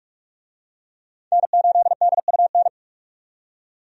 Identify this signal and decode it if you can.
Morse code, heard as a keyed tone, about 43 wpm, 700 Hz. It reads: N8DUN
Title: USA Call Signs - 5 Characters - 45wpm